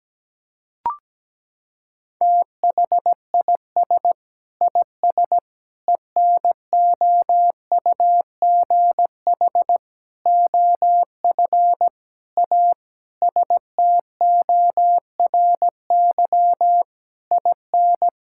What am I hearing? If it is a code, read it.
Morse code, heard as a keyed tone, about 17 wpm, 700 Hz. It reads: THIS IS ENOUGH OF A STORY IN
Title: Sentences from Top 1000 Words - Review Part 5 - 17wpm